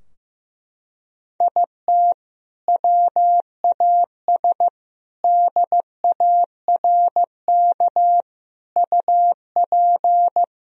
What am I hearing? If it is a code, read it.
Morse code, heard as a keyed tone, about 15 wpm, 700 Hz. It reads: IT WAS DARK UP
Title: Sentences from Top 1000 Words - Review Part 2 - 15wpm